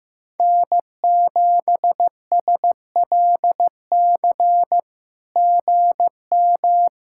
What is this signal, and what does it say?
Morse code, heard as a keyed tone, about 15 wpm, 700 Hz. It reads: N7SLC GM
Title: SOTA QSO Practice 15wpm